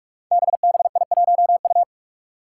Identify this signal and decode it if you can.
Morse code, heard as a keyed tone, about 44 wpm, 700 Hz. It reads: BBI1U
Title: Five Challenging Letter-Number Combinations 45wpm